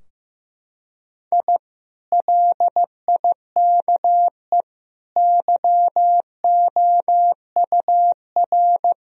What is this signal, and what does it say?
Morse code, heard as a keyed tone, about 15 wpm, 700 Hz. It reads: I LIKE YOUR
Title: Sentences from Top 1000 Words - Review Part 4 - 15wpm